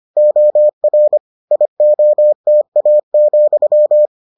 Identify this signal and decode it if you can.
Morse code, heard as a keyed tone, about 25 wpm, 600 Hz. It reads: OR IOTA,